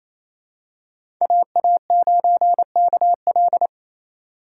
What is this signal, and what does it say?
Morse code, heard as a keyed tone, about 28 wpm, 700 Hz. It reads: AA9KL
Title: USA Call Signs - 5 Characters - 4x Character Spacing - 28wpm